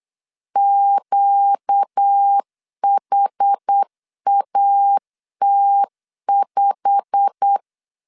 Morse code, heard as a keyed tone, about 8 wpm, 800 Hz. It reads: QHAT5
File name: LEZIONI / 8 YJ45 / 8 YJ45 8wpm 3214